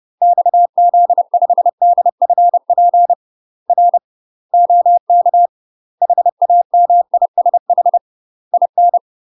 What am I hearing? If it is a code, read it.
Morse code, heard as a keyed tone, about 30 wpm, 700 Hz. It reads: XZ5DFP R OK HAMISH IN
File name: UK MASTER QSOs TRAINING / 30WPM / UK MASTER QSOs 0043